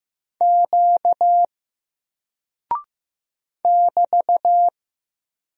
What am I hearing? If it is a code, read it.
Morse code, heard as a keyed tone, about 15 wpm, 700 Hz. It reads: Q <BT>
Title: Single Character (Letter:Number:Punctuation:Prosign) 15wpm